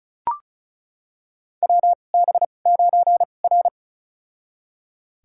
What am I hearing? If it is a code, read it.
Morse code, heard as a keyed tone, about 36 wpm, spaced as for 25 wpm, 700 Hz. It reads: WB9R
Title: USA Call Signs - 4 Characters - 2x Character Spacing - 35wpm